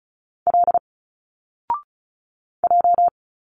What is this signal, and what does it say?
Morse code, heard as a keyed tone, about 34 wpm, 700 Hz. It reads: L J
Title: Single Letters 35wpm